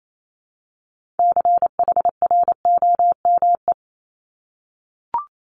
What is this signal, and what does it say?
Morse code, heard as a keyed tone, about 28 wpm, 700 Hz. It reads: CHROME
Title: Multi-letter Phonemes Repeated 3x 28wpm